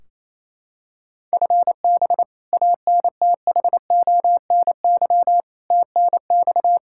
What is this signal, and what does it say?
Morse code, heard as a keyed tone, about 28 wpm, 700 Hz. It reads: FB ANTHONY TNX
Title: Practice QSO - Exchange 3 - More Info - 28wpm